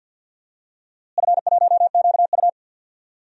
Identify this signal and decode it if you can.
Morse code, heard as a keyed tone, about 47 wpm, 700 Hz. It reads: F1QU
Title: DXCC-100 Call Signs - 4 Characters - Repeated 2x - 50wpm